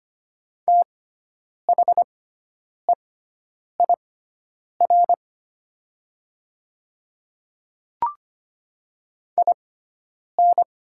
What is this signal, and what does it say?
Morse code, heard as a keyed tone, about 25 wpm, spaced as for 9 wpm, 700 Hz. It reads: THEIR IN
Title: 6x Character Spacing - Top 200 Words - 25wpm